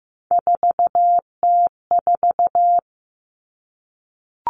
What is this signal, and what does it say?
Morse code, heard as a keyed tone, about 15 wpm, 700 Hz. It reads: 4T4
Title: Common Cut Numbers - 3-Digit 15wpm